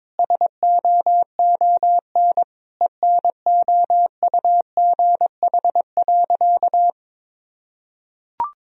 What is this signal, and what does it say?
Morse code, heard as a keyed tone, about 22 wpm, 700 Hz. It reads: SOON ENOUGH.
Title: Sentences from Top 1000 Words - Review Part 2 - 22wpm